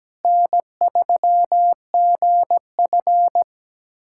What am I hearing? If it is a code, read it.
Morse code, heard as a keyed tone, about 17 wpm, 700 Hz. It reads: N3GF
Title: USA Callsign DE Callsign - 4 Characters - 17wpm